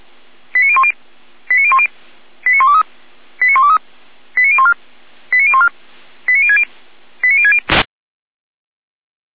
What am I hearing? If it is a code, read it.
Five-tone selective call (ZVEI), heard as quick runs of five tones, tones of about 70 ms. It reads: 89010 89010 89123 89123 89014 89014 89060 89060